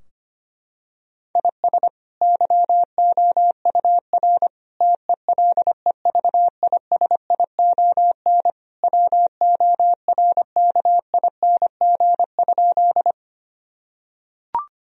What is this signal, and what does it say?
Morse code, heard as a keyed tone, about 25 wpm, 700 Hz. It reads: IS YOUR TELEVISION WORKING?
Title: Sentences from Top 1000 Words - Review Part 4 - 25wpm